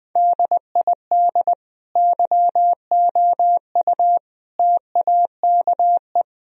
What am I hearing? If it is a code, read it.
Morse code, heard as a keyed tone, about 20 wpm, 700 Hz. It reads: DID YOU TAKE
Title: Sentences from Top 1000 Words - Review Part 4 - 20wpm